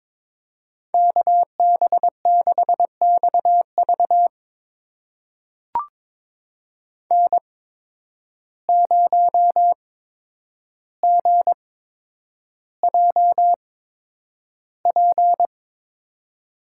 Morse code, heard as a keyed tone, about 22 wpm, spaced as for 9 wpm, 700 Hz. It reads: KB6XV N0GJP
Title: USA Call Signs - 5 Characters - 8x Character Spacing - 22wpm